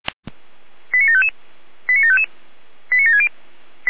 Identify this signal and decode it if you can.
Five-tone selective call (ZVEI), heard as quick runs of five tones, tones of about 80 ms. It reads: 89755 89755 89760